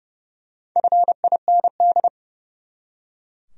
Morse code, heard as a keyed tone, about 31 wpm, 700 Hz. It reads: FIND
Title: Top 100 Words 30wpm